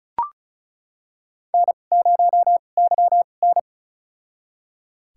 Morse code, heard as a keyed tone, about 34 wpm, spaced as for 25 wpm, 700 Hz. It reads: N0YN
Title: USA Call Signs - 2x Character Spacing - 35wpm